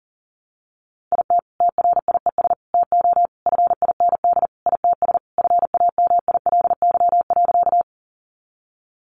Morse code, heard as a keyed tone, about 37 wpm, 700 Hz. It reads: IT TRIES TO FIND ITS FAMILY.
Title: Additional Easy Phrases 40wpm